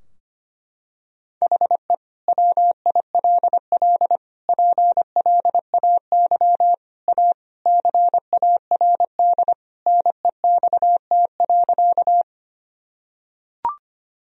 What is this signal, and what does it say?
Morse code, heard as a keyed tone, about 25 wpm, 700 Hz. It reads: HE WILL PLAY A CARD NEXT.
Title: Sentences from Top 1000 Words - Review Part 4 - 25wpm